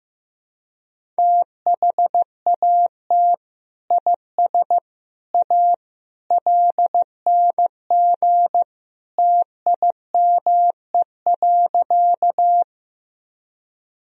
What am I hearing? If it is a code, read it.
Morse code, heard as a keyed tone, about 15 wpm, 700 Hz. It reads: THAT IS A LNG TIME.
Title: Sentences from Top 100 Words and Abbreviations 15wpm